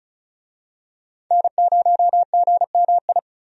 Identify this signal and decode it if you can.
Morse code, heard as a keyed tone, about 36 wpm, 700 Hz. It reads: N0GMI
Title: USA Call Signs - 5 Characters - 35wpm